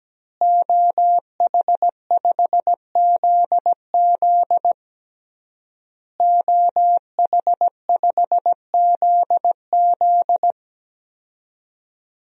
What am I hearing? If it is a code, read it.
Morse code, heard as a keyed tone, about 17 wpm, 700 Hz. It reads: OH5ZZ OH5ZZ
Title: DXCC-100 Call Signs - 5 Characters - Repeated 3x - 17wpm